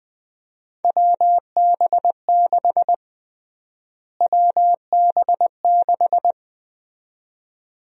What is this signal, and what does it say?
Morse code, heard as a keyed tone, about 21 wpm, 700 Hz. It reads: WB6 WB6
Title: USA Call Sign Prefixes - Repeated 2x - 20wpm